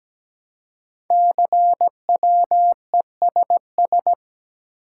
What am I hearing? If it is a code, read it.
Morse code, heard as a keyed tone, about 17 wpm, 700 Hz. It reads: CWESS
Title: Code Groups - 5 Letters - 17wpm